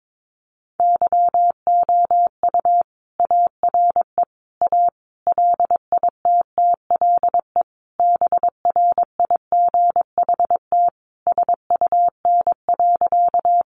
Morse code, heard as a keyed tone, about 22 wpm, 700 Hz. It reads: YOU ARE A LITTLE BRIGHT SUN.
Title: Sentences from Top 1000 Words 22wpm